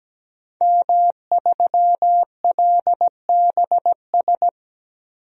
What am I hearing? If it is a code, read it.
Morse code, heard as a keyed tone, about 17 wpm, 700 Hz. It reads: M3LBS
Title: DXCC-100 Call Signs - 5 Characters - 17wpm